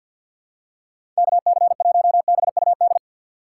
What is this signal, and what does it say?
Morse code, heard as a keyed tone, about 51 wpm, 700 Hz. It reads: KC1BUD
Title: USA Call Signs - 6 Characters - 3x Character Spacing - 50wpm